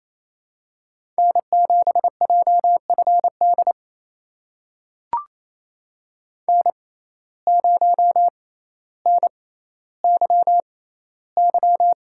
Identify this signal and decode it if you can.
Morse code, heard as a keyed tone, about 28 wpm, spaced as for 14 wpm, 700 Hz. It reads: N7JFD N0NYY
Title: USA Call Signs - 5 Characters - 6x Character Spacing - 28wpm